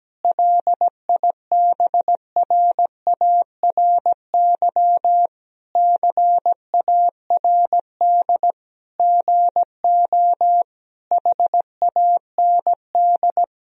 Morse code, heard as a keyed tone, about 17 wpm, 700 Hz. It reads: LIBRARY CARD GO HAND
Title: Sentences from Top 1000 Words - Review Part 4 - 17wpm